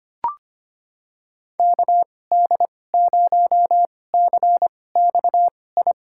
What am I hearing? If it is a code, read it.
Morse code, heard as a keyed tone, about 25 wpm, spaced as for 18 wpm, 700 Hz. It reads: KD0CXI
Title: USA Call Signs - 6 Characters - 2x Character Spacing - 25wpm